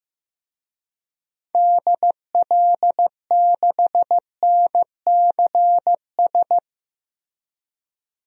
Morse code, heard as a keyed tone, about 15 wpm, 700 Hz. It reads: DL6NCS
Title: DXCC-100 Call Signs - 6 Characters - Repeated 2x - 15wpm